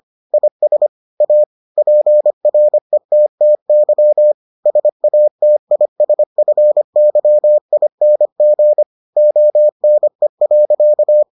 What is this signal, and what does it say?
Morse code, heard as a keyed tone, about 25 wpm, 600 Hz. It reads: IS A PRETTY SATISFYING ONE.